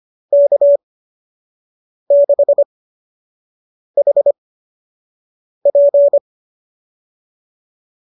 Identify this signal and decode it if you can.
Morse code, heard as a keyed tone, about 25 wpm, spaced as for 6 wpm, 590 Hz. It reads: K6HP
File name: qso003 06